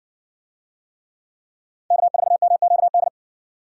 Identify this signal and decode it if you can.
Morse code, heard as a keyed tone, about 60 wpm, 700 Hz. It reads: K4MQD